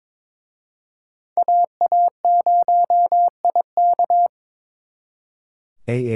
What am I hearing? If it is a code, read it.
Morse code, heard as a keyed tone, about 26 wpm, 700 Hz. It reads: AA0IK
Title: Speed Racing - USA 2x2 Call Signs 22wpm